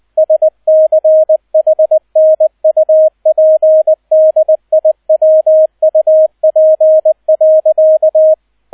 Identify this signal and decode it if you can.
Morse code, heard as a keyed tone, about 20 wpm, 630 Hz, with radio noise. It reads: SCHNUPDIWUP.
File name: 002 Zweiter Streich 20wpm